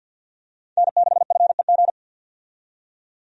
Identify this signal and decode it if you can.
Morse code, heard as a keyed tone, about 52 wpm, 700 Hz. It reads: N6FEG
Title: USA Call Signs - 5 Characters - 50wpm